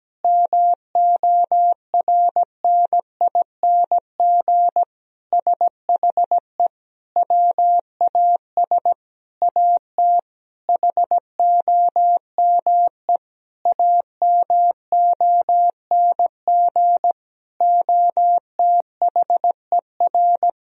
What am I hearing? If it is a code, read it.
Morse code, heard as a keyed tone, about 17 wpm, 700 Hz. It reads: MORNING SHE WAS AT HOME AMONG OTHER
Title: Sentences from Top 1000 Words - Review Part 3 - 17wpm